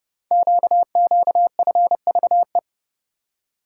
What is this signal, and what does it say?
Morse code, heard as a keyed tone, about 28 wpm, 700 Hz. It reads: QQFVE
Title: Code Groups - 5 Letter:Numbers - 30wpm